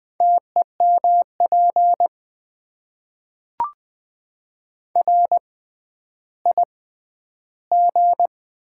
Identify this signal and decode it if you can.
Morse code, heard as a keyed tone, about 20 wpm, spaced as for 11 wpm, 700 Hz. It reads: TEMP RIG
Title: Top 90 QSO Elements - 6x Character Spacing - 20wpm